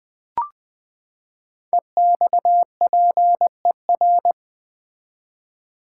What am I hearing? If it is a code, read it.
Morse code, heard as a keyed tone, about 20 wpm, 700 Hz. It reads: EXPER
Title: Build A Word - 5 Letters At A Time - 10-Letter Words - 20wpm